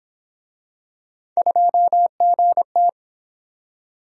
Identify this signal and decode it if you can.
Morse code, heard as a keyed tone, about 30 wpm, 700 Hz. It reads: 2GT